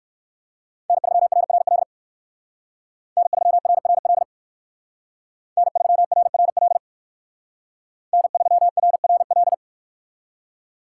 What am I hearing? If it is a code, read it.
Morse code, heard as a keyed tone, about 58 wpm, 700 Hz. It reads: N3RRL N3RRL N3RRL N3RRL